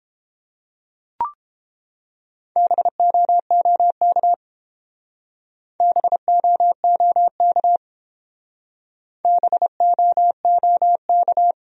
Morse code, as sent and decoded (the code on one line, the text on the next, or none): -... --- --- -.- / -... --- --- -.- / -... --- --- -.-
BOOK BOOK BOOK